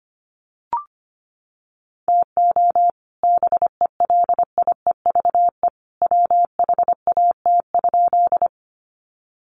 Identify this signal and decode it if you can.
Morse code, heard as a keyed tone, about 25 wpm, 700 Hz. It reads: TO BELIEVE WHAT?
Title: Sentences from Top 1000 Words 25wpm